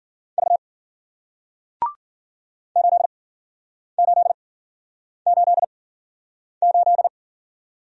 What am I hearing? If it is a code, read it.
Morse code, heard as a keyed tone, about 53 wpm, spaced as for 9 wpm, 700 Hz. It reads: V 8888